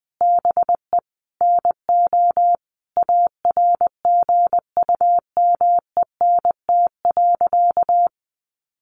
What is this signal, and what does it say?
Morse code, heard as a keyed tone, about 20 wpm, 700 Hz. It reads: BE NO ARGUMENT.
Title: A Sound Of Thunder 20wpm